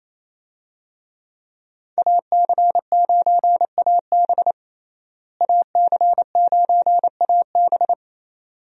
Morse code, heard as a keyed tone, about 27 wpm, 700 Hz. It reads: AC9AB AC9AB
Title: USA Call Signs - 5 Characters - Repeated 2x - 28wpm